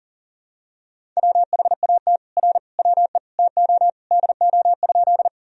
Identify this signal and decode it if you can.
Morse code, heard as a keyed tone, about 37 wpm, 700 Hz. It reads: WHAT R WE TO DO?
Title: Sentences from Top 100 Words and Abbreviations 40wpm